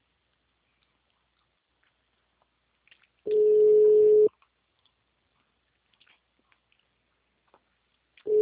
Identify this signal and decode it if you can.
Ringback tone (Europe, 425 Hz).
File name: out- -2020